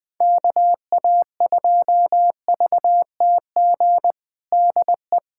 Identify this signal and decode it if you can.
Morse code, heard as a keyed tone, about 20 wpm, 700 Hz. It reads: KA2VTG DE
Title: USA Callsign DE Callsign - 6 Characters - 20wpm